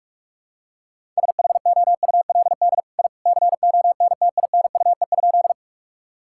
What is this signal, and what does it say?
Morse code, heard as a keyed tone, about 46 wpm, 700 Hz. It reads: SHOULD I CONTINUE?
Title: Sentences from Top 1000 Words - Review Part 6 - 45wpm